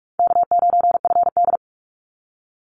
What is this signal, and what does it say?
Morse code, heard as a keyed tone, about 44 wpm, 700 Hz. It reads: K9FD
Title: Easy Call Signs Repeated 3x - 45wpm